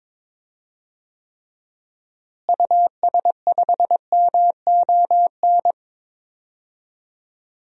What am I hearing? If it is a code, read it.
Morse code, heard as a keyed tone, about 22 wpm, 700 Hz. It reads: US5MON
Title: DXCC-100 Call Signs - 6 Characters - Repeated 3x - 22wpm